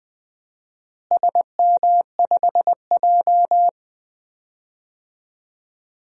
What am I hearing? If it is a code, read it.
Morse code, heard as a keyed tone, about 20 wpm, 700 Hz. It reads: SM5J
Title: DXCC-100 Call Signs - 4 Characters - Repeated 3x - 20wpm